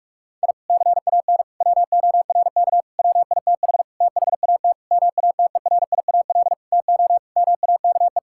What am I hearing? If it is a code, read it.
Morse code, heard as a keyed tone, about 46 wpm, 700 Hz. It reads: I CAN WORK WITH THAT MATERIAL TO MAKE
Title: Sentences from Top 1000 Words - Review Part 5 - 45wpm